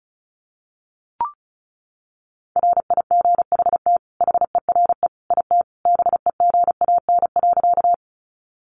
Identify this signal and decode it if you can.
Morse code, heard as a keyed tone, about 34 wpm, 700 Hz. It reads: RIGHT HERE IT BEGAN.
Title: Sentences from Top 300 Words 35wpm